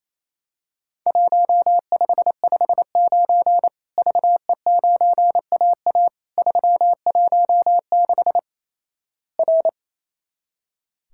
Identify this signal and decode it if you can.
Morse code, heard as a keyed tone, about 28 wpm, 700 Hz. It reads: 1559 VE9AA 316 R
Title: Worked All Europe DX Contest - QTC Exchange - 28wpm